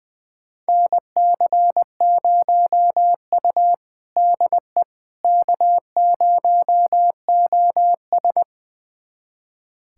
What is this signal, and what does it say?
Morse code, heard as a keyed tone, about 20 wpm, 700 Hz. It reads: NC0U DE K0OS
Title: USA Callsign DE Callsign - 4 Characters - 20wpm